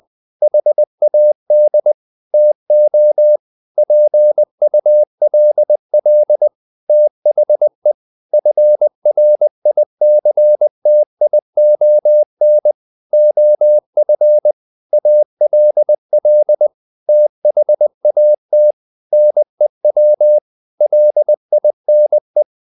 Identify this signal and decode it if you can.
Morse code, heard as a keyed tone, about 20 wpm, 600 Hz. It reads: HAD TO PULL THE FRICTION OF ALL THAT NEW LINE
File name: the old man and the sea part 2 20wpm